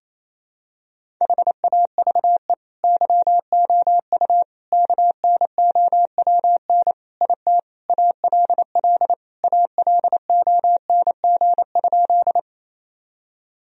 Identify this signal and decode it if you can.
Morse code, heard as a keyed tone, about 28 wpm, 700 Hz. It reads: HAVE YOU KNOWN IT ALL ALONG?
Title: Sentences from Top 1000 Words - Review Part 3 - 28wpm